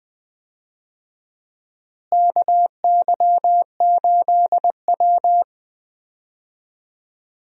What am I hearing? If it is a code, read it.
Morse code, heard as a keyed tone, about 20 wpm, 700 Hz. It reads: KY8W
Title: USA Call Signs - 4 Characters - Repeated 3x - 20wpm